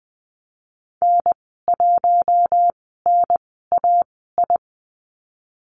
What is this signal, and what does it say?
Morse code, heard as a keyed tone, about 19 wpm, spaced as for 14 wpm, 700 Hz. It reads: N1NAI
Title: 055 Lesson 10g - Letter r - Call Signs 20wpm